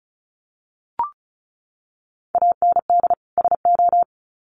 Morse code, heard as a keyed tone, about 34 wpm, 700 Hz. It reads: AND SO
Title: Sets of 2 Words 35wpm